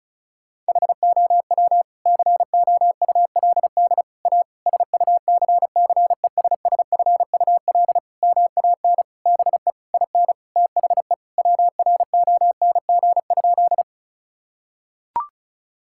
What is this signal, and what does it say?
Morse code, heard as a keyed tone, about 34 wpm, 700 Hz. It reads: HOW COULD A SUCCESSFUL MAN BE IN THE WRONG?
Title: Sentences from Top 1000 Words - Review Part 6 - 35wpm